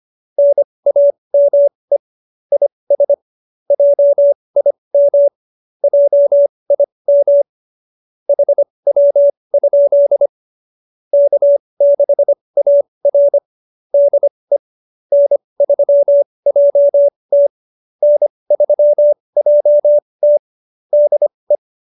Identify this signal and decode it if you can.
Morse code, heard as a keyed tone, about 25 wpm, spaced as for 20 wpm, 570 Hz. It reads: NAME IS JIM JIM HW? K6AR DE N3JT N3JT DE
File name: QSO307 20